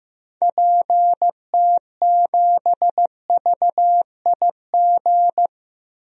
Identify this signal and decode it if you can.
Morse code, heard as a keyed tone, about 16 wpm, 700 Hz. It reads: PT7VIG
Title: DXCC-100 Call Signs - 6 Characters - Repeated 3x - 15wpm